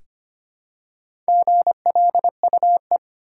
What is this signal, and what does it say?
Morse code, heard as a keyed tone, about 25 wpm, 700 Hz. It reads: GLUE
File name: practice / 25wpm / Easy Words with Hints - 25wpm